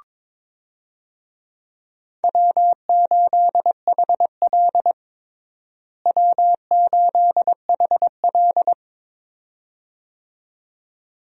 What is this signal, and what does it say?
Morse code, heard as a keyed tone, about 21 wpm, 700 Hz. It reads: W8HL W8HL
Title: USA Call Signs - 4 Characters - Repeated 2x - 22wpm